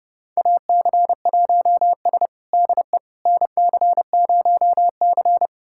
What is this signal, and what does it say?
Morse code, heard as a keyed tone, about 29 wpm, 700 Hz. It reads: AC1S DE NC0C
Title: USA Callsign DE Callsign - 4 Characters - 30wpm